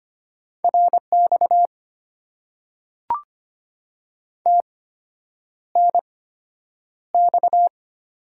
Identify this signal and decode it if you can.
Morse code, heard as a keyed tone, about 25 wpm, spaced as for 9 wpm, 700 Hz. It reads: RX TNX
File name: Top 90 QSO Elements - 8x Character Spacing - 25wpm